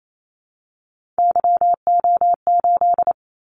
Morse code, heard as a keyed tone, about 28 wpm, 700 Hz. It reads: YO8
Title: Callsign Prefixes for Top 25 DXCC - 28wpm